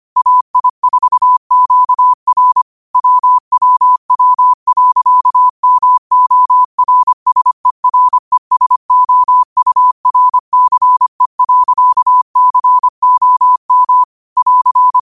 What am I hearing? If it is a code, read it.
Morse code, heard as a keyed tone, about 25 wpm, 1000 Hz. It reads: AI4QR WWW.MORSERESOURCE.COM <AR>
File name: qotdfiles / 25wpm